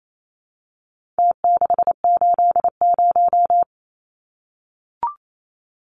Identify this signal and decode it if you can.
Morse code, heard as a keyed tone, about 28 wpm, 700 Hz. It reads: T680
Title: Numbers - Contest Style - 4-Digit 28wpm